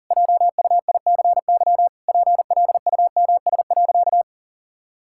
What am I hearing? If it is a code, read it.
Morse code, heard as a keyed tone, about 37 wpm, 700 Hz. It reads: JUICY PLUMS.
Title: Pangram Challenge v2 - 40wpm